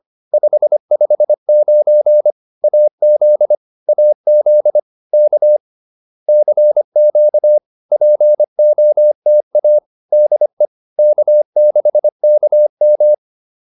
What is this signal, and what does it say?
Morse code, heard as a keyed tone, about 25 wpm, 600 Hz. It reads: 559 AZ AZ K CQ POTA DE K6KM